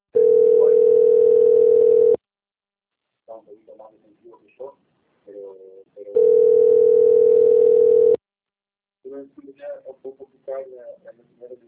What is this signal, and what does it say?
Ringback tone (North America), 440 + 480 Hz, with speech.
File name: out- - - - - .21848